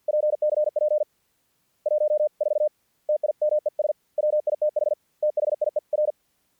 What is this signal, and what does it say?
Morse code, heard as a keyed tone, about 50 wpm, 600 Hz. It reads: PCP 1 4 TIMES WITH THIER
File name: archived / CRSnet